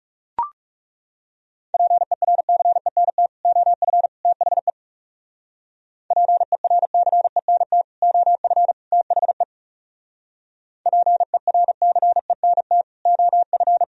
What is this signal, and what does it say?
Morse code, heard as a keyed tone, about 36 wpm, 700 Hz. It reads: PERCENT OF THE PERCENT OF THE PERCENT OF
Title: Speed-Racing - Sets of 3 Words 30wpm